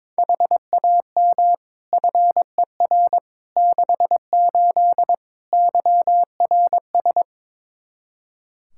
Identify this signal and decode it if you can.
Morse code, heard as a keyed tone, about 21 wpm, 700 Hz. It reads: HAM FER 68 YRS
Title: K1USN SST QSO Practice 22wpm